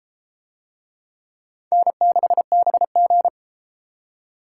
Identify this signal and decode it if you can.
Morse code, heard as a keyed tone, about 33 wpm, 700 Hz. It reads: N6BG